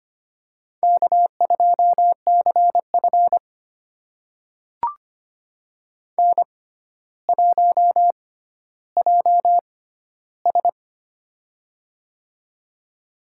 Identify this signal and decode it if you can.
Morse code, heard as a keyed tone, about 25 wpm, spaced as for 13 wpm, 700 Hz. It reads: K2CF N1JS
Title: USA Call Signs - 4 Characters - 6x Character Spacing - 25wpm